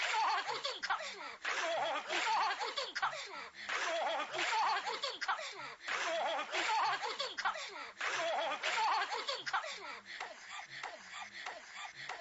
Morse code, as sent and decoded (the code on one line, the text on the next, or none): none